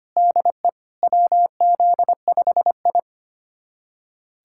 Morse code, heard as a keyed tone, about 25 wpm, 700 Hz. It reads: DE WZ5I
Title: USA Callsign DE Callsign - 4 Characters - 25wpm